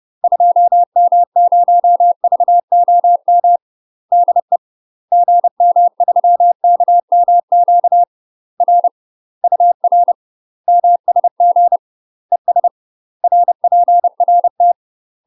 Morse code, heard as a keyed tone, about 30 wpm, 700 Hz. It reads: 2M0VOM DE GM3KMQ R UR MSG ES RPRT
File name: UK MASTER QSOs TRAINING / 30WPM / UK MASTER QSOs 0010